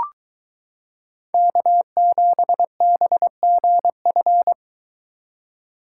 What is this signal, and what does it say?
Morse code, heard as a keyed tone, about 22 wpm, 700 Hz. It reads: K7BGF